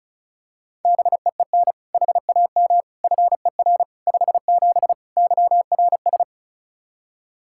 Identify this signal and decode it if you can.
Morse code, heard as a keyed tone, about 36 wpm, 700 Hz. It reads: BEEN HAM FER 57 YRS
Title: K1USN SST QSO Practice 35wpm